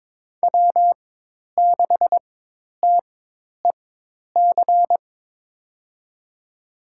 Morse code, heard as a keyed tone, about 22 wpm, spaced as for 10 wpm, 700 Hz. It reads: W6TEC
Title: USA Call Signs - 5 Characters - 4x Character Spacing - 22wpm